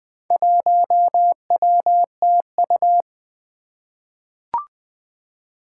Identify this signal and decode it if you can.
Morse code, heard as a keyed tone, about 20 wpm, 700 Hz. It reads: 1WTU
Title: Four Challenging Letter-Number Combinations 20wpm